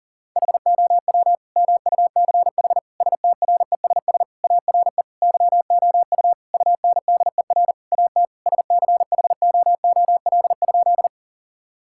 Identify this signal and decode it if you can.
Morse code, heard as a keyed tone, about 37 wpm, 700 Hz. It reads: HOW MUCH STRESS ARE YOU UNDER AT SCHOOL?
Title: Sentences from Top 1000 Words - Review Part 6 - 40wpm